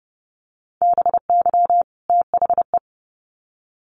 Morse code, heard as a keyed tone, about 30 wpm, 700 Hz. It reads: BY THE
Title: Sets of 2 Words 30wpm